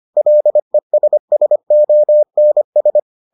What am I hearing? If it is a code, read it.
Morse code, heard as a keyed tone, about 25 wpm, 600 Hz. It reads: LESSONS